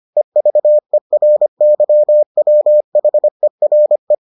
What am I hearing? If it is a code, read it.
Morse code, heard as a keyed tone, about 25 wpm, 600 Hz. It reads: EVERYWHERE